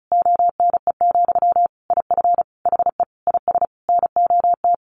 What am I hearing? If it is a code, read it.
Morse code, heard as a keyed tone, about 34 wpm, 700 Hz. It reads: ONE, IF HE IS NOT
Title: Sentences from Top 900 Words 35wpm